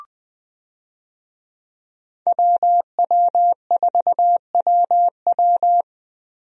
Morse code, heard as a keyed tone, about 21 wpm, 700 Hz. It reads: WW4WW
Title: USA Call Signs - 5 Characters - Repeated 2x - 20wpm